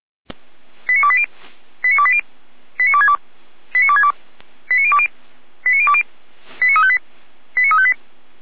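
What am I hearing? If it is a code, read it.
Five-tone selective call (ZVEI), heard as quick runs of five tones, tones of about 70 ms. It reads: 89280 89280 89262 89262 89020 89020 89358 89358
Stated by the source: ZVEI five-tone call 89333 received on an SDR.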